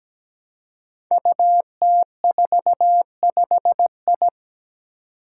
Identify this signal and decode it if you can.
Morse code, heard as a keyed tone, about 17 wpm, 700 Hz. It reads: UT45I
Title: Five Challenging Letter-Number Combinations 17wpm